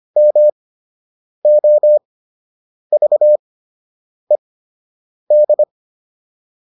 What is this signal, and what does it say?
Morse code, heard as a keyed tone, about 25 wpm, spaced as for 8 wpm, 600 Hz. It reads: MOVED